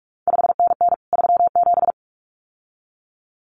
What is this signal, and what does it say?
Morse code, heard as a keyed tone, about 46 wpm, 700 Hz. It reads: 5NN 37
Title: CQ WW DX Contest Exchange Encore - 45wpm